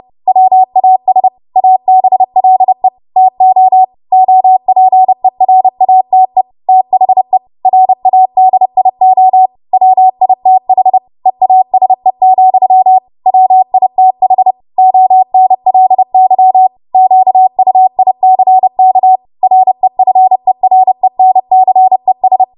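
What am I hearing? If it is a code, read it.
Morse code, heard as a keyed tone, about 30 wpm, 750 Hz. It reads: WAS ABLE TO OPERATE THE RADIO WITH EASE, WITH ONLY QUICK REFERENCES
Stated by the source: Morse practice transmission at 30 WPM.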